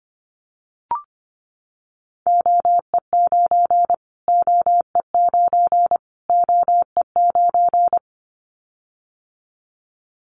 Morse code, heard as a keyed tone, about 24 wpm, 700 Hz. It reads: OE9 OE9 OE9
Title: Callsign Prefixes for Top 25 DXCC - Repeated 3x - 25wpm